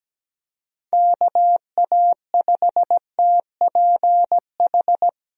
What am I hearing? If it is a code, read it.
Morse code, heard as a keyed tone, about 17 wpm, 700 Hz. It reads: KA5TPH
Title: USA Call Signs - 6 Characters - 17wpm